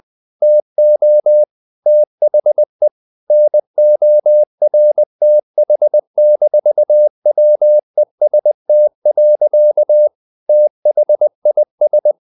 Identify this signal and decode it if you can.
Morse code, heard as a keyed tone, about 20 wpm, 600 Hz. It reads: TO THE NORTH-WEST. THIS
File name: the old man and the sea / the old man and the sea part 1 20wpm